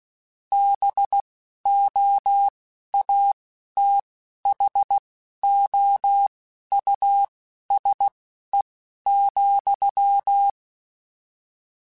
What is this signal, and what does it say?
Morse code, heard as a keyed tone, about 16 wpm, spaced as for 11 wpm, 800 Hz. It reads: BOATHOUSE,
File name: General / General-mp3s / Houses @10